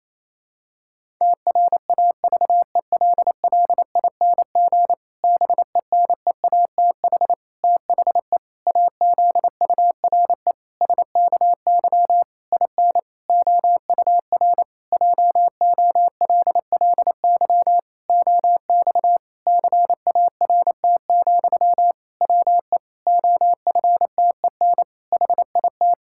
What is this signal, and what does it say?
Morse code, heard as a keyed tone, about 27 wpm, 700 Hz. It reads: TRAVELLING BENEATH THE AZURE SKY IN OUR JOLLY OX CART, WE OFTEN HIT
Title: Pangram Challenge v2 - 28wpm